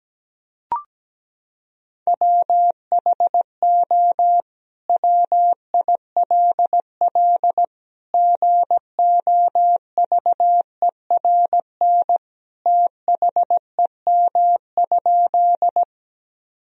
Morse code, heard as a keyed tone, about 17 wpm, 700 Hz. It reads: WHO WILL GOVERN THEM?
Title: Sentences from Top 1000 Words - Review Part 2 - 17wpm